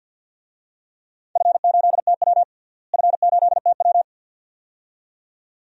Morse code, heard as a keyed tone, about 52 wpm, 700 Hz. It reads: F8TW F8TW
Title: DXCC-100 Call Signs - 4 Characters - Repeated 2x - 50wpm